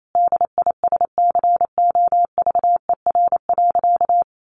Morse code, heard as a keyed tone, about 28 wpm, 700 Hz. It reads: DISCOVER.
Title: Sentences from Top 1100 Words 28wpm